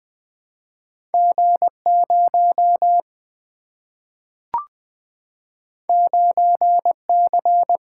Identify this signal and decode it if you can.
Morse code, heard as a keyed tone, about 20 wpm, 700 Hz. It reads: G 0 9 C
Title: Single-Letter Single-Number 20wpm